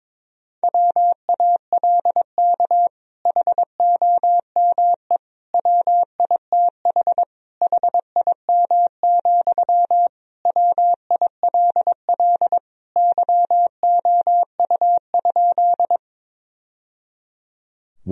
Morse code, heard as a keyed tone, about 22 wpm, 700 Hz. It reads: WALK HOME WITH HIM, WILL YOU?
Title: Sentences from Top 1000 Words - Review Part 1 - 22wpm